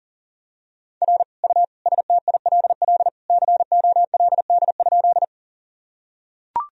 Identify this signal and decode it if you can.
Morse code, heard as a keyed tone, about 37 wpm, 700 Hz. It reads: R U STILL COLD?
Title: Sentences from Top 400 Words and Abbreviations 40wpm